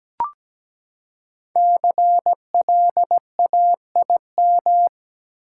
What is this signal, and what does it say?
Morse code, heard as a keyed tone, about 17 wpm, 700 Hz. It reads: CLAIM
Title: Five Letter Words - 17wpm